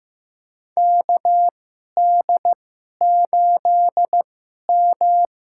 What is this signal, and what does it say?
Morse code, heard as a keyed tone, about 15 wpm, spaced as for 11 wpm, 700 Hz. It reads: KD8M
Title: USA Call Signs - 4 Characters - 2x Character Spacing - 15wpm